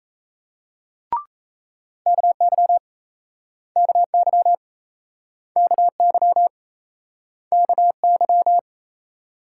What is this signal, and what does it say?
Morse code, heard as a keyed tone, about 40 wpm, 700 Hz. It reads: KY KY KY KY